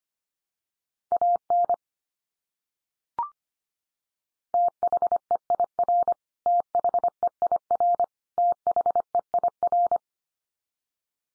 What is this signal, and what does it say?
Morse code, heard as a keyed tone, about 25 wpm, 700 Hz. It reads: AN THEIR THEIR THEIR
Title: Top 100 Words 25wpm x3